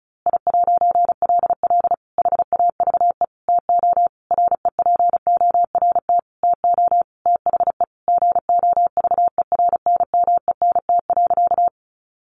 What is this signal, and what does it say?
Morse code, heard as a keyed tone, about 34 wpm, 700 Hz. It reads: I'LL HAVE TO REPORT TO THE GOVERNMENT.
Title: A Sound Of Thunder 35wpm